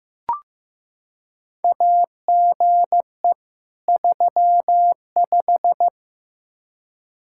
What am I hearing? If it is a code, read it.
Morse code, heard as a keyed tone, about 15 wpm, 700 Hz. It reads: AGE 35
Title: Practice QSO - Exchange 3 - More Info - 15wpm